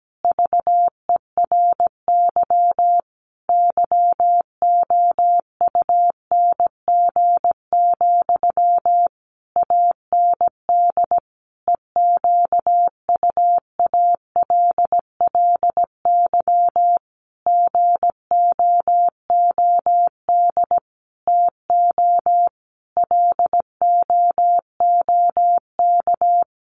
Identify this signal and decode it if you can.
Morse code, heard as a keyed tone, about 17 wpm, 700 Hz. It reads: VERY YOUNG, AND EQUALLY GOOD TO LOOK
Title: Sentences from Top 1400 Words 17wpm